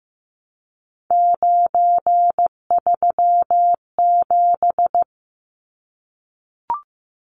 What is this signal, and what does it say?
Morse code, heard as a keyed tone, about 15 wpm, 700 Hz. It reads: 937
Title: Numbers - 3-Digit 15wpm